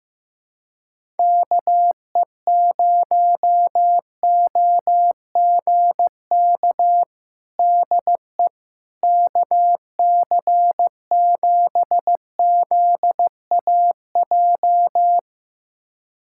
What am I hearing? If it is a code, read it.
Morse code, heard as a keyed tone, about 15 wpm, 700 Hz. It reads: KE0OGK DE KC7ZAJ
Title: USA Callsign DE Callsign - 6 Characters - 15wpm